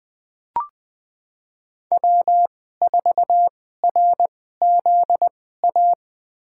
Morse code, heard as a keyed tone, about 21 wpm, spaced as for 15 wpm, 700 Hz. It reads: W4RZA
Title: USA Call Signs - 2x Character Spacing - 20wpm